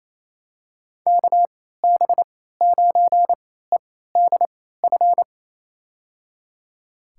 Morse code, heard as a keyed tone, about 28 wpm, spaced as for 16 wpm, 700 Hz. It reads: KB9EDF
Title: USA Call Signs - 6 Characters - 3x Character Spacing - 28wpm